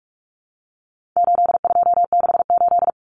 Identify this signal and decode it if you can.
Morse code, heard as a keyed tone, about 44 wpm, 700 Hz. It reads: 8268
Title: Numbers - 4-Digit 45wpm